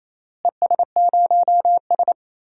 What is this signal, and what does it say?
Morse code, heard as a keyed tone, about 28 wpm, 700 Hz. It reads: ES0S
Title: DXCC-100 Call Signs - 4 Characters - Repeated 3x - 28wpm